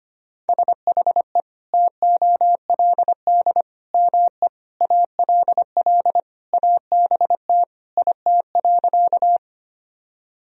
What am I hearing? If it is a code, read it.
Morse code, heard as a keyed tone, about 25 wpm, 700 Hz. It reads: SHE TOLD ME ALL ABT IT.
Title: Sentences from Top 400 Words and Abbreviations 25wpm